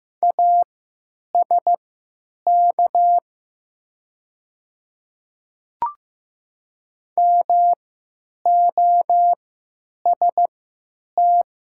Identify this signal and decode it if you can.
Morse code, heard as a keyed tone, about 15 wpm, spaced as for 9 wpm, 700 Hz. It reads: ASK MOST
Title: 3x Character Spacing - Top 200 Words - 15wpm